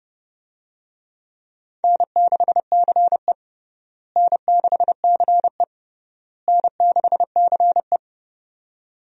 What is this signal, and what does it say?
Morse code, heard as a keyed tone, about 30 wpm, 700 Hz. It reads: N6CE N6CE N6CE
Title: USA Call Signs - 4 Characters - Repeated 3x - 30wpm